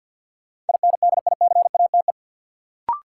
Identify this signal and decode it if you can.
Morse code, heard as a keyed tone, about 53 wpm, 700 Hz. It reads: INDICATE
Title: Sentences from Top 1300 Words - New Word Drill - 50wpm